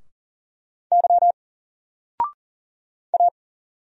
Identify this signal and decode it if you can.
Morse code, heard as a keyed tone, about 39 wpm, 700 Hz. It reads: Y A
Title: Single Letter-Number 40wpm